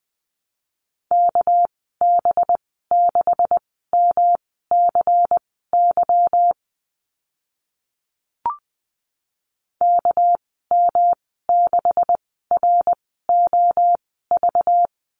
Morse code, heard as a keyed tone, about 19 wpm, spaced as for 14 wpm, 700 Hz. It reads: KB6MCY KM6ROV
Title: 213 Lesson 34g - Number 8 - Call Signs 20wpm